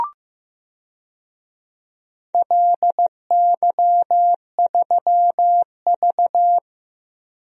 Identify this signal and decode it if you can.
Morse code, heard as a keyed tone, about 16 wpm, 700 Hz. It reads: LY3V
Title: DXCC-100 Call Signs - 4 Characters - Repeated 2x - 15wpm